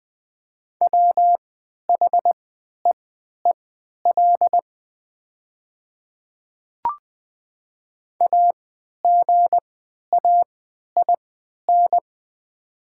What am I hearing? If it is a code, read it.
Morse code, heard as a keyed tone, about 20 wpm, spaced as for 11 wpm, 700 Hz. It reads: WHEEL AGAIN
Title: 3x Character Spacing - Top 500 Words - 20wpm